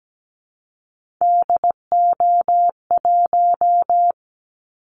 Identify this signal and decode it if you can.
Morse code, heard as a keyed tone, about 17 wpm, 700 Hz. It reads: DO1
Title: Callsign Prefixes for Top 50 DXCC - 17wpm